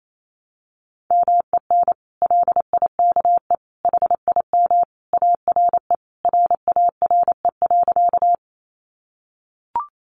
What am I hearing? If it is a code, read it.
Morse code, heard as a keyed tone, about 28 wpm, 700 Hz. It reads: MEN LIKE HIM ARE RARE.
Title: Sentences from Top 1100 Words 28wpm